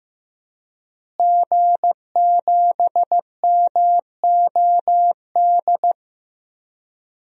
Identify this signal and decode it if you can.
Morse code, heard as a keyed tone, about 16 wpm, 700 Hz. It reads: G7MOD
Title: DXCC-100 Call Signs - 5 Characters - Repeated 3x - 15wpm